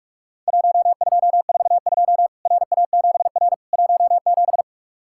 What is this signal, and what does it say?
Morse code, heard as a keyed tone, about 44 wpm, 700 Hz. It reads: 1242 RA7R 17
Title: Worked All Europe DX Contest - QTC Exchange - 45wpm